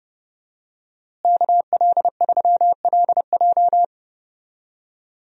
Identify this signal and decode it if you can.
Morse code, heard as a keyed tone, about 30 wpm, 700 Hz. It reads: KL3LJ
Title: USA Call Signs - 5 Characters - Repeated 2x - 30wpm